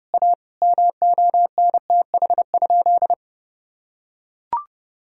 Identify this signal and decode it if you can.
Morse code, heard as a keyed tone, about 29 wpm, 700 Hz. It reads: A MONTH?
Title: Sentences from Top 1000 Words - Review Part 6 - 30wpm